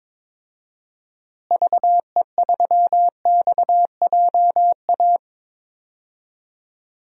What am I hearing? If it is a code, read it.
Morse code, heard as a keyed tone, about 22 wpm, 700 Hz. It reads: VE3XJA
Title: DXCC-100 Call Signs - 6 Characters - Repeated 2x - 22wpm